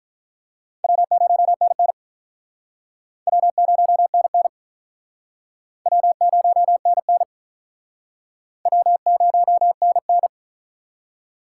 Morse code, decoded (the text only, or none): W0NN W0NN W0NN W0NN